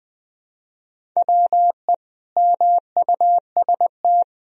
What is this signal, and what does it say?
Morse code, heard as a keyed tone, about 20 wpm, 700 Hz. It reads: WE MUST
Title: Sentences from Top 1000 Words - Review Part 1 - 20wpm